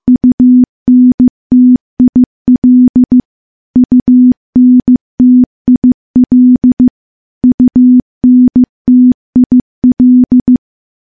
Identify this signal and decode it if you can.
Morse code, heard as a keyed tone, about 16 wpm, 250 Hz. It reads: UNTIL UNTIL UNTIL
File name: morse / morse-training-15WPM